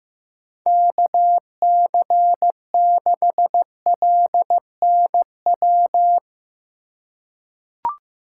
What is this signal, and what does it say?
Morse code, heard as a keyed tone, about 15 wpm, 700 Hz. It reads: KC6LNW
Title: USA Call Signs - 6 Characters - 6x Character Spacing - 15wpm